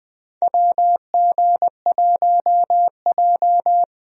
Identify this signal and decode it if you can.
Morse code, heard as a keyed tone, about 20 wpm, 700 Hz. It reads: WG1J
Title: USA Call Signs - 4 Characters - Repeated 3x - 20wpm